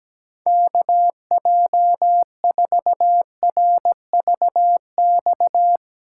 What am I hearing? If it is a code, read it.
Morse code, heard as a keyed tone, about 17 wpm, 700 Hz. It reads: KJ4RVX
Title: USA Callsign DE Callsign - 6 Characters - 17wpm